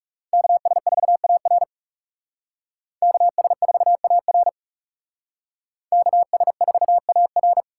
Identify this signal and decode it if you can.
Morse code, heard as a keyed tone, about 37 wpm, 700 Hz. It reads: KS4AR KS4AR KS4AR
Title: Speed Racing - USA 2x2 Call Signs 30wpm